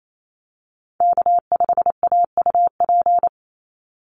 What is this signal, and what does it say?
Morse code, heard as a keyed tone, about 28 wpm, 700 Hz. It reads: K5AUP
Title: Call Signs Repeated 2x - 28wpm